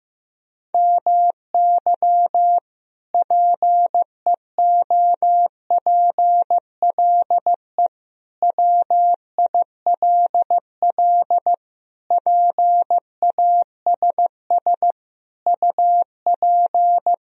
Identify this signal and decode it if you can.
Morse code, heard as a keyed tone, about 15 wpm, 700 Hz. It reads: MY PEOPLE WILL PASS UP
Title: Sentences from Top 1000 Words - Review Part 2 - 15wpm